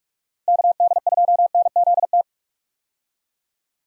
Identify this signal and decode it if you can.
Morse code, heard as a keyed tone, about 46 wpm, 700 Hz. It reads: KD2NZT
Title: USA Call Signs - 6 Characters - Repeated 2x - 45wpm